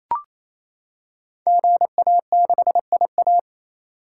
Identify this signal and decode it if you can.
Morse code, heard as a keyed tone, about 27 wpm, 700 Hz. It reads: GA6IA
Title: Code Groups - 5 Letter:Numbers - 28wpm